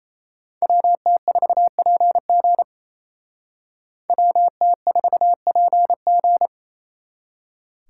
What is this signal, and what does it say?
Morse code, heard as a keyed tone, about 36 wpm, 700 Hz. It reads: WT4PG WT4PG